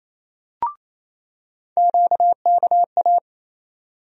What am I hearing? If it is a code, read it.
Morse code, heard as a keyed tone, about 28 wpm, 700 Hz. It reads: QKA
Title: USA Call Sign Postfixes - 3 Letters 28wpm